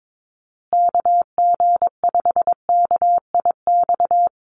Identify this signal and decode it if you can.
Morse code, heard as a keyed tone, about 22 wpm, 700 Hz. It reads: KG5KIX
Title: USA Call Signs Repeated 2x 22wpm